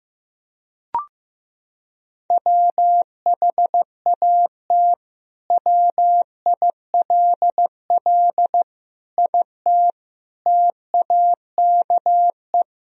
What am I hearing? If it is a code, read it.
Morse code, heard as a keyed tone, about 15 wpm, 700 Hz. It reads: WHAT WILL IT TAKE
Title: Sentences from Top 1000 Words - Review Part 3 - 15wpm